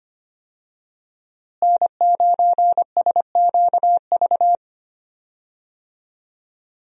Morse code, heard as a keyed tone, about 25 wpm, 700 Hz. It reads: N9SQV
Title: USA Call Signs - 5 Characters - Repeated 2x - 25wpm